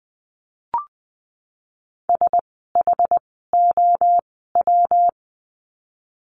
Morse code, heard as a keyed tone, about 20 wpm, 700 Hz. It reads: S H O W
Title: 206 Lesson 33f - Letter j - Words 20wpm